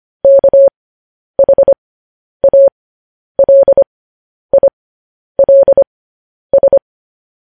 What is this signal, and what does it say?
Morse code, heard as a keyed tone, about 25 wpm, spaced as for 10 wpm, 550 Hz. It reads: KHALILS
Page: news headlines in morse code at 10 wpm